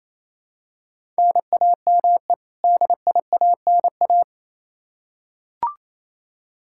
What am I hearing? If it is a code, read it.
Morse code, heard as a keyed tone, about 28 wpm, 700 Hz. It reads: NAME DIANA
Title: Practice QSO - Exchange 1 - Beginning - 28wpm